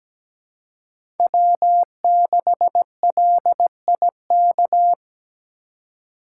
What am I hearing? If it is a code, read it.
Morse code, heard as a keyed tone, about 17 wpm, 700 Hz. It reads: W6LIK
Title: USA Call Signs - 3x Character Spacing - 17wpm